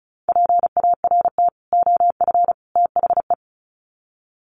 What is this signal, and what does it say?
Morse code, heard as a keyed tone, about 34 wpm, 700 Hz. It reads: PART OF THE
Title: Sets of 3 Words 35wpm